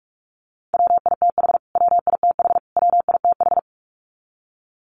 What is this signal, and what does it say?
Morse code, heard as a keyed tone, about 44 wpm, 700 Hz. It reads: WITH WITH WITH
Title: Top 500 Words Repeated 45wpm